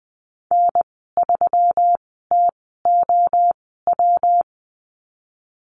Morse code, heard as a keyed tone, about 20 wpm, spaced as for 14 wpm, 700 Hz. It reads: N3TOW
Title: 122 Lesson 20g - Number 3 - Call Signs 20wpm